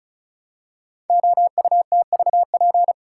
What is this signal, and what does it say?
Morse code, heard as a keyed tone, about 36 wpm, 690 Hz. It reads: OUTVP
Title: Code Groups - 5 Letter:Numbers - 35wpm